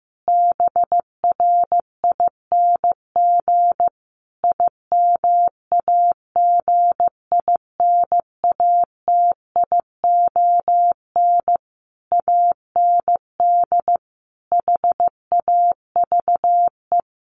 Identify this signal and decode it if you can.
Morse code, heard as a keyed tone, about 15 wpm, 700 Hz. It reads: BRING IMAGINATION AND HAVE
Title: Sentences from Top 1300 Words 15wpm